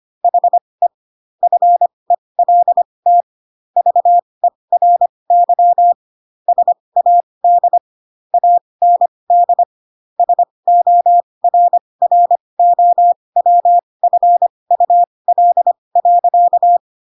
Morse code, heard as a keyed tone, about 25 wpm, spaced as for 20 wpm, 700 Hz. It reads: HE FELT VERY SAD AND SORROWFUL.
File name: Pinocchio / Pinocchio ch7 15 25 ik1hge